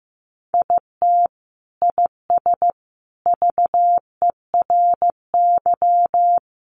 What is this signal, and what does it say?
Morse code, heard as a keyed tone, about 15 wpm, 700 Hz. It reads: IT IS VERY
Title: Sentences from Top 200 Words 15wpm